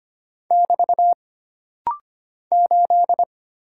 Morse code, heard as a keyed tone, about 25 wpm, 700 Hz. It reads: <BT> 8
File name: Single Character (Letter:Number:Punctuation:Prosign) - Rapid-Fire - 25wpm